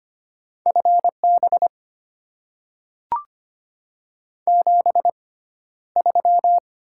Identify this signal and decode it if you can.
Morse code, heard as a keyed tone, about 25 wpm, spaced as for 13 wpm, 700 Hz. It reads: FB 73
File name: Top 20 QSO Elements - 6x Character Spacing - 25wpm